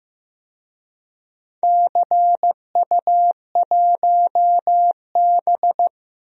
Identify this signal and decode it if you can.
Morse code, heard as a keyed tone, about 15 wpm, 700 Hz. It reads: CU1B
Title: DXCC-100 Call Signs - 4 Characters - Repeated 3x - 15wpm